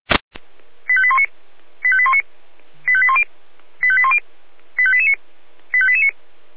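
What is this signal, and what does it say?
Five-tone selective call (ZVEI), heard as quick runs of five tones, tones of about 70 ms. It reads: 85819 85819 85810 85810 85889 85889
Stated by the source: ZVEI five-tone selective call received on SDR.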